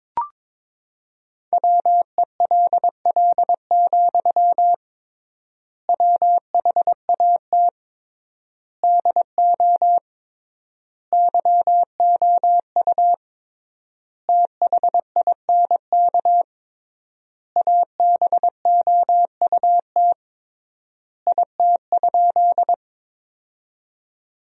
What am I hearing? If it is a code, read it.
Morse code, heard as a keyed tone, about 22 wpm, 700 Hz. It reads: WELL, WHAT DO YOU THINK ABOUT IT?
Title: 3x Word-Spacing - Sentences from Top 200 Words - 22wpm